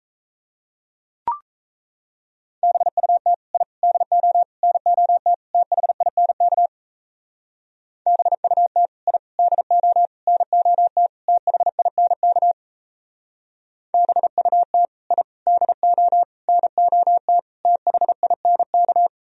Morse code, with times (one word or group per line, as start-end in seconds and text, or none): 2.6-3.3 BUT
3.5-3.6 I
3.8-4.4 DO
4.6-5.3 NOT
5.5-6.7 THINK
8.1-8.9 BUT
9.1-9.2 I
9.4-10.1 DO
10.3-11.1 NOT
11.3-12.5 THINK
13.9-14.8 BUT
15.1-15.2 I
15.5-16.2 DO
16.5-17.4 NOT
17.7-19.1 THINK